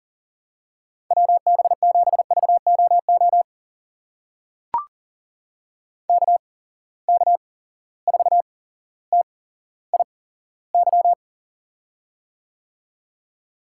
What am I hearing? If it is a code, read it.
Morse code, heard as a keyed tone, about 43 wpm, spaced as for 17 wpm, 700 Hz. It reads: WB7VOO KK4TIY
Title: USA Call Signs - 6 Characters - 8x Character Spacing - 40wpm